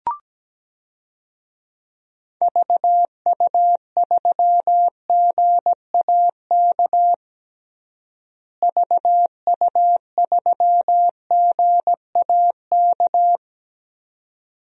Morse code, heard as a keyed tone, about 17 wpm, 700 Hz. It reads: VU3GAK VU3GAK
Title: DXCC-100 Call Signs - 6 Characters - Repeated 2x - 17wpm